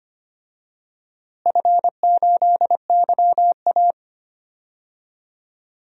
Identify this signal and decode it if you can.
Morse code, heard as a keyed tone, about 25 wpm, 700 Hz. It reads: F8YA
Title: DXCC-100 Call Signs - 4 Characters - Repeated 2x - 25wpm